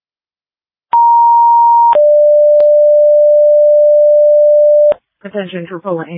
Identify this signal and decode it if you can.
Two-tone page, heard as a one-second tone, then a three-second tone, about 940 then 590 Hz, with speech. It is heard from 0.9 to 4.9 s.